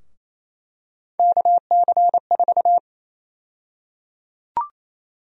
Morse code, heard as a keyed tone, about 28 wpm, 700 Hz. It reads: KC4
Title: USA Call Sign Prefixes - Repeated 2x - 28wpm